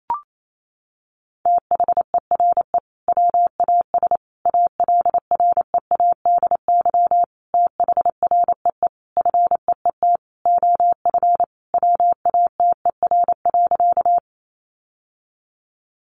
Morse code, heard as a keyed tone, about 28 wpm, 700 Hz. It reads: THERE WAS ALREADY THREE FEET OF WATER.
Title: Sentences from Top 700 Words 28wpm